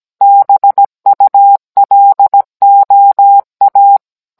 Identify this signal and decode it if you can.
Morse code, heard as a keyed tone, about 17 wpm, 800 Hz. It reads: BULOA